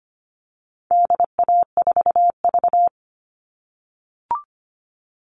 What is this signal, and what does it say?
Morse code, heard as a keyed tone, about 25 wpm, 700 Hz. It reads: DA4V
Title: All Cut Numbers - 4-Digit 25wpm